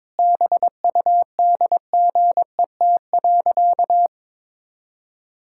Morse code, heard as a keyed tone, about 22 wpm, 700 Hz. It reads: BUDGET.
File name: practice / 22wpm / Sentences with 8-Word Phrases - 22wpm